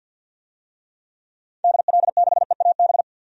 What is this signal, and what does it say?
Morse code, heard as a keyed tone, about 48 wpm, 700 Hz. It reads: DL6EAB
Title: DXCC-100 Call Signs - 6 Characters - 50wpm